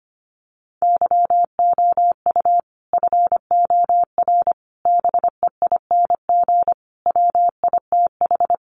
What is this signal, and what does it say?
Morse code, heard as a keyed tone, about 25 wpm, 700 Hz. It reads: YOU FOR BEING WITH
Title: Sets of 5 Words 25wpm